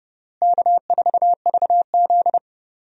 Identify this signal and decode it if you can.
Morse code, heard as a keyed tone, about 30 wpm, 700 Hz. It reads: K4VZ
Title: USA Call Signs - 4 Characters - Repeated 3x - 30wpm